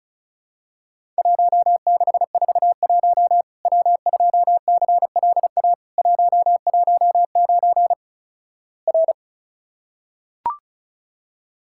Morse code, heard as a keyed tone, about 36 wpm, 700 Hz. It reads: 1641 W2CLA 119 R
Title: Worked All Europe DX Contest - QTC Exchange - 35wpm